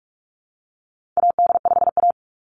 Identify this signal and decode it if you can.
Morse code, heard as a keyed tone, about 44 wpm, 700 Hz. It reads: AD5A
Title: Call Signs Repeated 3x - 45wpm